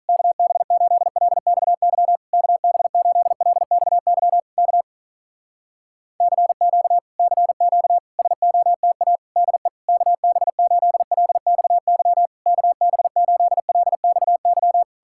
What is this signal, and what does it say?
Morse code, heard as a keyed tone, about 46 wpm, 700 Hz. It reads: KB8LXY KB8LXY K CQ CQ SOTA DE KB8LXY KB8LXY